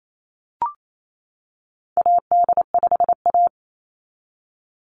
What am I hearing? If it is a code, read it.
Morse code, heard as a keyed tone, about 28 wpm, 700 Hz. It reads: AD5A
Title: NA SOTA Chaser Callsigns 28wpm